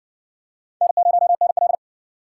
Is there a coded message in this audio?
Yes: Morse code from 0.8 to 1.8 s, heard as a keyed tone, about 59 wpm, 700 Hz. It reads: N0NL